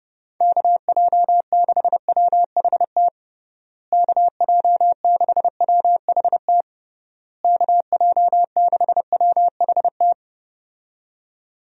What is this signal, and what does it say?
Morse code, heard as a keyed tone, about 30 wpm, 700 Hz. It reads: KJ6WHT KJ6WHT KJ6WHT
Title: USA Call Signs - 6 Characters - Repeated 3x - 30wpm